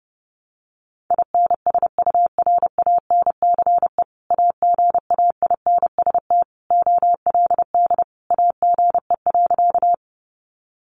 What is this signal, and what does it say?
Morse code, heard as a keyed tone, about 28 wpm, 700 Hz. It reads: INSURANCE AGAINST OLD AGE.
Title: Sentences from Top 1100 Words 30wpm